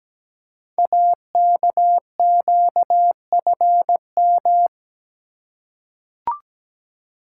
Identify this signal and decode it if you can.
Morse code, heard as a keyed tone, about 17 wpm, 690 Hz. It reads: AKQFM
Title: Code Groups - 5 Letter:Numbers - 17wpm